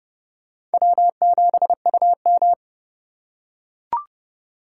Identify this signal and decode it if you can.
Morse code, heard as a keyed tone, about 32 wpm, 700 Hz. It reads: W7UM
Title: USA Call Signs - 4 Characters - 2x Character Spacing - 30wpm